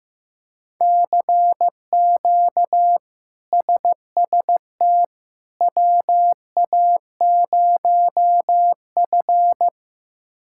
Morse code, heard as a keyed tone, about 15 wpm, 700 Hz. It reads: CQ SST WA0F
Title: K1USN SST QSO Practice - 15wpm